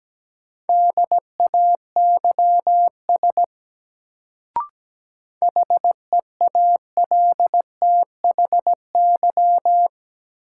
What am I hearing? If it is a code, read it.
Morse code, heard as a keyed tone, about 17 wpm, 700 Hz. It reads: DAYS HEALTHY
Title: Sentences from Top 1000 Words - Review Part 5 - New Word Drill - 17wpm